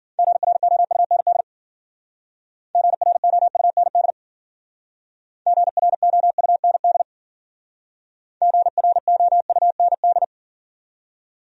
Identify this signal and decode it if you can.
Morse code, heard as a keyed tone, about 52 wpm, 700 Hz. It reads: GROUND GROUND GROUND GROUND